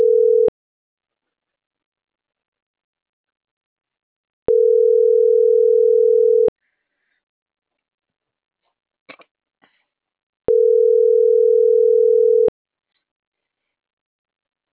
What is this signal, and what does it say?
Ringback tone (North America), 440 + 480 Hz.